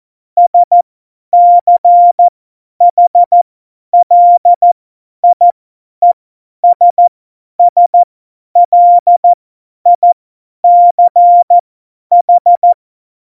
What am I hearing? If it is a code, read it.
Morse code, heard as a keyed tone, about 14 wpm, spaced as for 10 wpm, 700 Hz. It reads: SCHLIESSLICH
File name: Morsekurs / Witze / Witz 09. Januar 2018, 50 F70